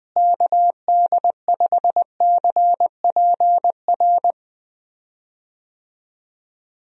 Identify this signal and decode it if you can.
Morse code, heard as a keyed tone, about 20 wpm, 700 Hz. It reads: KD5CPR
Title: USA Call Signs - 6 Characters - Repeated 2x - 20wpm